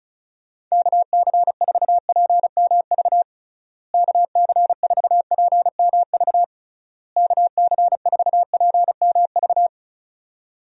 Morse code, heard as a keyed tone, about 36 wpm, 700 Hz. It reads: KC4PMV KC4PMV KC4PMV
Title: USA Call Signs - 6 Characters - Repeated 3x - 35wpm